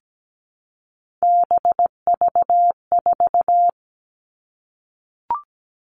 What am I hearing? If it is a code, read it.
Morse code, heard as a keyed tone, about 17 wpm, 700 Hz. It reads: BV4
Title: All Cut Numbers - 3-Digit 17wpm